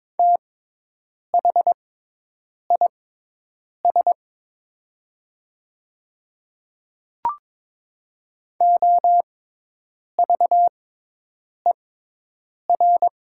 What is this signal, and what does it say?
Morse code, heard as a keyed tone, about 22 wpm, spaced as for 8 wpm, 700 Hz. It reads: THIS OVER
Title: 6x Character Spacing - Top 100 Words - 22wpm